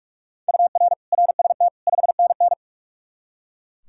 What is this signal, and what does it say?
Morse code, heard as a keyed tone, about 44 wpm, 700 Hz. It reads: UR RST 5NN
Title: K1USN SST QSO Practice 45wpm